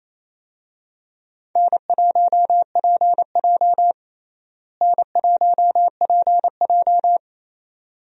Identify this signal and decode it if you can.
Morse code, heard as a keyed tone, about 28 wpm, 700 Hz. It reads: N1PJ N1PJ
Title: USA Call Signs - 4 Characters - Repeated 2x - 28wpm